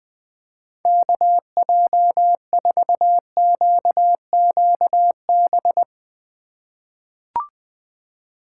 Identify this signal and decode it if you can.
Morse code, heard as a keyed tone, about 20 wpm, 700 Hz. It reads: KJ4QQB
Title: USA Call Signs - 2x Character Spacing - 20wpm